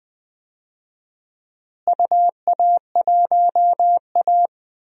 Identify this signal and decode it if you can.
Morse code, heard as a keyed tone, about 20 wpm, 700 Hz. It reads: UA1A
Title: DXCC-100 Call Signs - 4 Characters - Repeated 2x - 20wpm